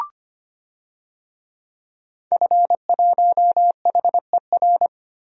Morse code, heard as a keyed tone, about 25 wpm, 700 Hz. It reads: F1HER
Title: DXCC-100 Call Signs - 5 Characters - 25wpm